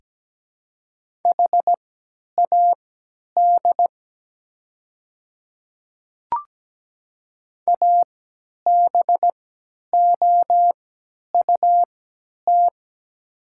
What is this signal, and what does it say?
Morse code, heard as a keyed tone, about 17 wpm, spaced as for 10 wpm, 700 Hz. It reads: HAD ABOUT
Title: 3x Character Spacing - Top 100 Words - 17wpm